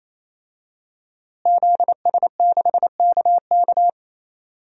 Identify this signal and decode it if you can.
Morse code, heard as a keyed tone, about 28 wpm, 700 Hz. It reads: ZS6KK
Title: DXCC-100 Call Signs - 5 Characters - Repeated 3x - 28wpm